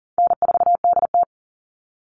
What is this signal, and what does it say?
Morse code, heard as a keyed tone, about 40 wpm, 700 Hz. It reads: N4DT
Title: Call Signs Repeated 2x - 40wpm